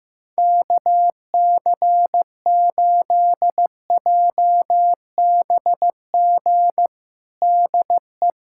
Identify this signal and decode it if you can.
Morse code, heard as a keyed tone, about 15 wpm, 700 Hz. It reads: KC8JBG DE
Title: USA Callsign DE Callsign - 6 Characters - 15wpm